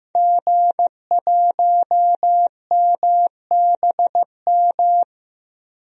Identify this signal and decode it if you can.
Morse code, heard as a keyed tone, about 15 wpm, 700 Hz. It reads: G1MBM
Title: DXCC-100 Call Signs - 5 Characters - Repeated 2x - 15wpm